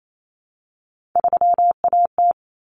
Morse code, heard as a keyed tone, about 28 wpm, 700 Hz. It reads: 3AT
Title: Common Cut Numbers - 3-Digit 28wpm